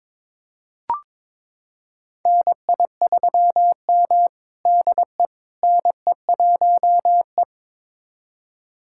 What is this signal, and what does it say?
Morse code, heard as a keyed tone, about 22 wpm, 700 Hz. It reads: NI3M DE NE1E
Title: USA Callsign DE Callsign - 4 Characters - 22wpm